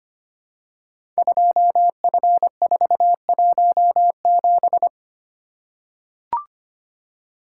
Morse code, heard as a keyed tone, about 25 wpm, 700 Hz. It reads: 2F417
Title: Code Groups - 5 Letter:Numbers - 25wpm